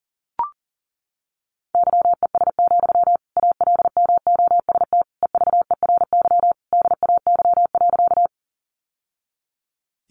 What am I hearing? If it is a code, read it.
Morse code, heard as a keyed tone, about 40 wpm, 700 Hz. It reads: YES, ALMOST EVERY DAY.
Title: Sentences from Top 700 Words 40wpm